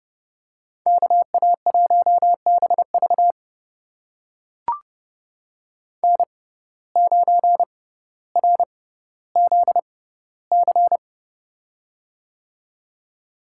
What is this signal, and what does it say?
Morse code, heard as a keyed tone, about 30 wpm, spaced as for 15 wpm, 700 Hz. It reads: KA1BV N9RZC
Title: USA Call Signs - 5 Characters - 6x Character Spacing - 30wpm